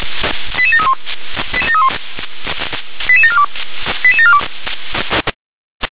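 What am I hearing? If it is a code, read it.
Five-tone selective call (ZVEI), heard as quick runs of five tones, tones of about 70 ms. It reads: 88731 88742 88742
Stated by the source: ZVEI five-tone call 88083, received by SDR.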